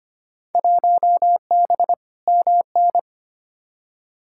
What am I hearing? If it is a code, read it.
Morse code, heard as a keyed tone, about 25 wpm, 700 Hz. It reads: 1B MN
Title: Field Day Exchanges Encore - 25wpm